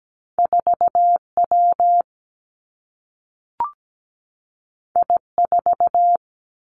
Morse code, heard as a keyed tone, about 17 wpm, 700 Hz. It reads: 4W I4
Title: Two Challenging Letter-Number Combinations 17wpm